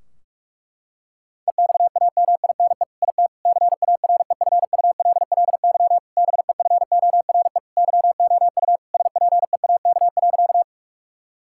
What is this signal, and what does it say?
Morse code, heard as a keyed tone, about 46 wpm, 700 Hz. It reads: EXAMINE IT CAREFULLY BEFORE YOU SPEAK.
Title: Sentences from Top 1600 Words 45wpm